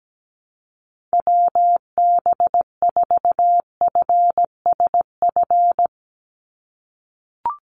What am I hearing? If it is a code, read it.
Morse code, heard as a keyed tone, about 17 wpm, 700 Hz. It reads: WB4FSF
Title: Call Signs - 17wpm